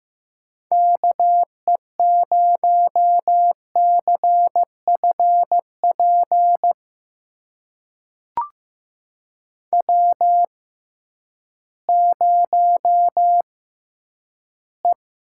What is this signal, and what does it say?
Morse code, heard as a keyed tone, about 15 wpm, spaced as for 10 wpm, 700 Hz. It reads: KE0CFP W0E
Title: USA Call Signs - 6x Character Spacing - 15wpm